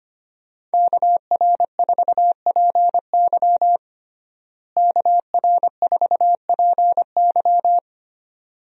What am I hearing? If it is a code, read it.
Morse code, heard as a keyed tone, about 25 wpm, 700 Hz. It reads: KR4PY KR4PY
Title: USA Call Signs - 5 Characters - Repeated 2x - 25wpm